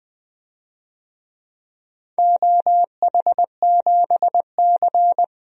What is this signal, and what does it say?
Morse code, heard as a keyed tone, about 20 wpm, 700 Hz. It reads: OH7C
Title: DXCC-100 Call Signs - 4 Characters - Repeated 3x - 20wpm